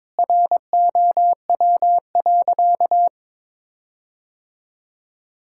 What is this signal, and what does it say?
Morse code, heard as a keyed tone, about 22 wpm, 700 Hz. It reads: ROW.
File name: practice / 22wpm / Sentences with 7-Word Phrases - 22wpm